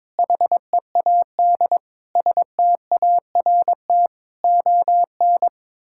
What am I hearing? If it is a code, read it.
Morse code, heard as a keyed tone, about 22 wpm, 700 Hz. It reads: HEAD START ON
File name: practice / 22wpm / Sets of 7 Words - 22wpm